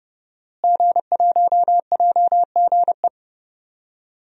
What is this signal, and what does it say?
Morse code, heard as a keyed tone, about 30 wpm, 700 Hz. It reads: G1JGE
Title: DXCC-100 Call Signs - 5 Characters - Repeated 3x - 30wpm